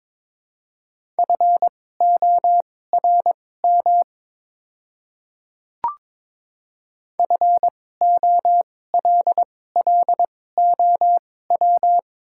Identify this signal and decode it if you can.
Morse code, heard as a keyed tone, about 22 wpm, spaced as for 16 wpm, 700 Hz. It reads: FORM FOLLOW
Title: 2x Character Spacing - Top 200 Words - 22wpm